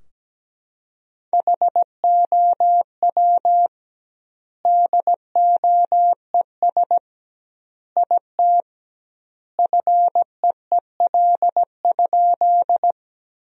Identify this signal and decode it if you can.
Morse code, heard as a keyed tone, about 17 wpm, 700 Hz. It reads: HOW DOES IT FEEL?
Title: 2x Word-Spacing - Sentences from Top 400 Words - 17wpm